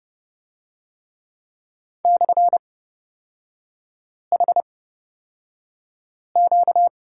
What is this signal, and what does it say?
Morse code, heard as a keyed tone, about 30 wpm, spaced as for 5 wpm, 700 Hz. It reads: /HQ
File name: Lesson 38e - Letter q - Three Characters 30wpm